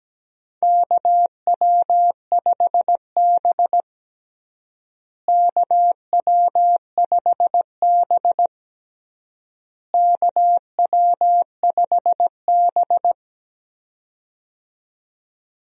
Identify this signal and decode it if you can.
Morse code, heard as a keyed tone, about 17 wpm, 700 Hz. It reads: KW5B KW5B KW5B
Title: USA Call Signs - 4 Characters - Repeated 3x - 17wpm